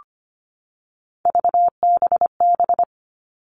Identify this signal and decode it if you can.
Morse code, heard as a keyed tone, about 25 wpm, 700 Hz. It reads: VBB
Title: Three Challenging Letter-Number Combinations 25wpm